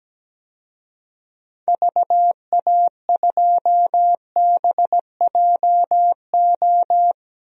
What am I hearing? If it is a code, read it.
Morse code, heard as a keyed tone, about 17 wpm, 700 Hz. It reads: VA2BJO
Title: DXCC-100 Call Signs - 6 Characters - 17wpm